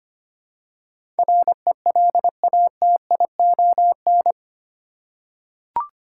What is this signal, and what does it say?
Morse code, heard as a keyed tone, about 25 wpm, 700 Hz. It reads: RELATION
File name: 8 Letter Words - First 3 Letters Given - 25wpm